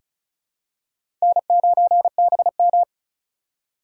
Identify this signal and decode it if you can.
Morse code, heard as a keyed tone, about 36 wpm, 700 Hz. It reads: N9BM
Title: USA Call Signs - 4 Characters - 4x Character Spacing - 35wpm